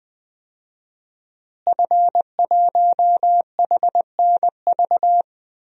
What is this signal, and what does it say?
Morse code, heard as a keyed tone, about 20 wpm, 700 Hz. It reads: F1HNV
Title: DXCC-100 Call Signs - 5 Characters - Repeated 2x - 20wpm